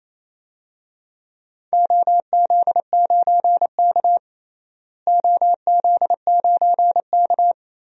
Morse code, heard as a keyed tone, about 28 wpm, 700 Hz. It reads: OZ9K OZ9K
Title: DXCC-100 Call Signs - 4 Characters - Repeated 3x - 28wpm